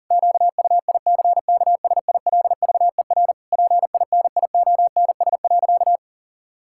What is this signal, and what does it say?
Morse code, heard as a keyed tone, about 37 wpm, 700 Hz. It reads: QUICKSILVER PINIONS.
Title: Pangram Challenge v2 - 40wpm